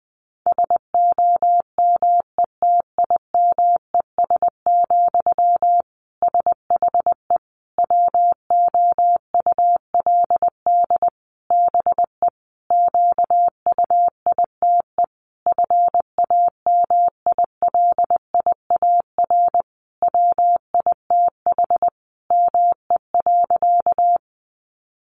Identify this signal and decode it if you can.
Morse code, heard as a keyed tone, about 19 wpm, 700 Hz. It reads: SOMETIMES, SHE WOULD BE QUITE FAMILIAR WITH ME.
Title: Sentences from Top 1000 Words 20wpm